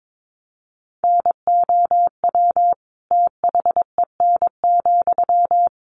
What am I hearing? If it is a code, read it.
Morse code, heard as a keyed tone, about 22 wpm, 700 Hz. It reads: NOW THEN,
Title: Sentences from Top 1100 Words 22wpm